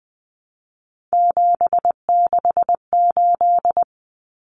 Morse code, heard as a keyed tone, about 20 wpm, 700 Hz. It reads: 768
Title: Common Cut Numbers - 3-Digit 20wpm